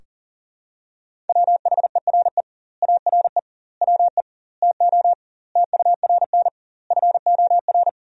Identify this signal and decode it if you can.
Morse code, heard as a keyed tone, about 40 wpm, 700 Hz. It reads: WHERE ARE WE TO TURN FOR
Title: 2x Word-Spacing - Sentences from Top 200 Words - 40wpm